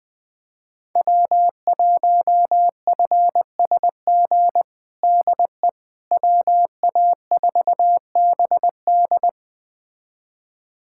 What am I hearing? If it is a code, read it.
Morse code, heard as a keyed tone, about 21 wpm, 700 Hz. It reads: W1FSG DE WA4BD
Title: USA Callsign DE Callsign - 5 Characters - 20wpm